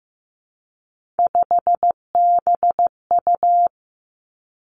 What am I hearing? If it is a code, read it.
Morse code, heard as a keyed tone, about 15 wpm, 700 Hz. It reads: 5BU
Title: Three Challenging Letter-Number Combinations 15wpm